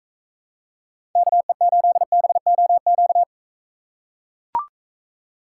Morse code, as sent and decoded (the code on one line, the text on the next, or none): -.- . ---.. -... --- --.-
KE8BOQ